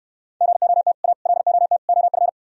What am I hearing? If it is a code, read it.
Morse code, heard as a keyed tone, about 67 wpm, 700 Hz. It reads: GOT A LOT OF